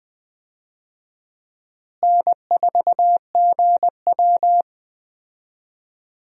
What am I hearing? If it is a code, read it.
Morse code, heard as a keyed tone, about 20 wpm, 700 Hz. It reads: N4GW
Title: USA Call Signs - 4 Characters - Repeated 2x - 20wpm